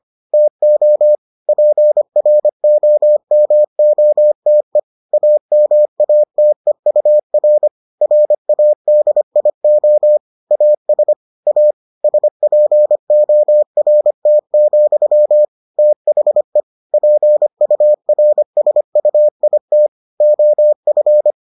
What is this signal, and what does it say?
Morse code, heard as a keyed tone, about 25 wpm, 600 Hz. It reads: TO PROMOTE AMATEUR RADIO AS A SPORT, THE PURSUIT OF